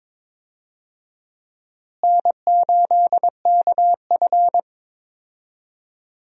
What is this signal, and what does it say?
Morse code, heard as a keyed tone, about 22 wpm, 700 Hz. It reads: N8KF
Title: USA Call Signs - 4 Characters - Repeated 2x - 22wpm